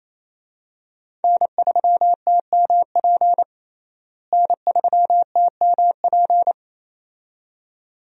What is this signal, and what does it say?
Morse code, heard as a keyed tone, about 28 wpm, 700 Hz. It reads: N3TMP N3TMP
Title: USA Call Signs - 5 Characters - Repeated 2x - 28wpm